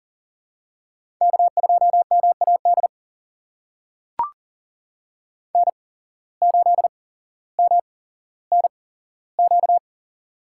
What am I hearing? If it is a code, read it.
Morse code, heard as a keyed tone, about 40 wpm, spaced as for 17 wpm, 700 Hz. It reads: K2MAD N8MNQ
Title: USA Call Signs - 5 Characters - 8x Character Spacing - 40wpm